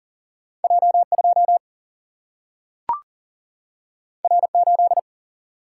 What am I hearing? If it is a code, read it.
Morse code, heard as a keyed tone, about 39 wpm, 700 Hz. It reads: J2 R8
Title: Single-Letter Single-Number 40wpm